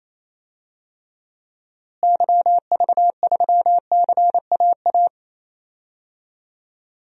Morse code, heard as a keyed tone, about 28 wpm, 700 Hz. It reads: YV3CAA
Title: DXCC-100 Call Signs - 6 Characters - Repeated 2x - 28wpm